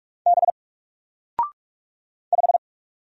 Morse code, heard as a keyed tone, about 44 wpm, 700 Hz. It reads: B 5
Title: Single Letter-Number 45wpm